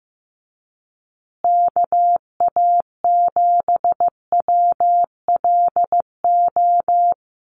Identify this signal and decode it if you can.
Morse code, heard as a keyed tone, about 15 wpm, 700 Hz. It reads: KA7WLO
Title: USA Call Signs Repeated 2x 15wpm